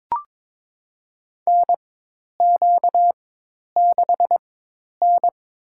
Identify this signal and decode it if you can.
Morse code, heard as a keyed tone, about 22 wpm, spaced as for 10 wpm, 700 Hz. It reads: NQ6N
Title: USA Call Signs - 4x Character Spacing - 22wpm